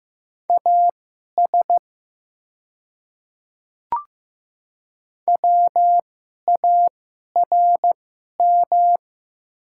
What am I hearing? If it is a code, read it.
Morse code, heard as a keyed tone, about 15 wpm, spaced as for 11 wpm, 690 Hz. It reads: AS WARM
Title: 2x Character Spacing - Top 500 Words - 15wpm